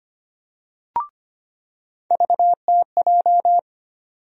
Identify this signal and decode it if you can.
Morse code, heard as a keyed tone, about 25 wpm, 700 Hz. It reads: VTJ
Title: USA Call Sign Postfixes - 3 Letters 25wpm